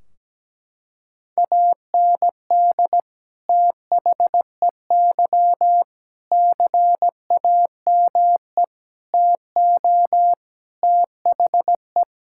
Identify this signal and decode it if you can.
Morse code, heard as a keyed tone, about 17 wpm, 700 Hz. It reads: AND THEY CAME TO THE
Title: Sentences from Top 1600 Words 17wpm